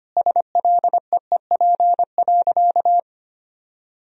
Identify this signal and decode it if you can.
Morse code, heard as a keyed tone, about 25 wpm, 700 Hz. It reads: SLEEP.
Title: Sentences from Top 1000 Words - Review Part 1 - 25wpm